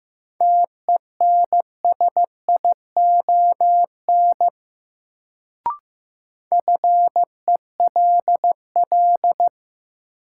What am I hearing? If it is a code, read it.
Morse code, heard as a keyed tone, about 16 wpm, 700 Hz. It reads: TENSION FELL
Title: Sentences from Top 1600 Words - New Word Drill - 15wpm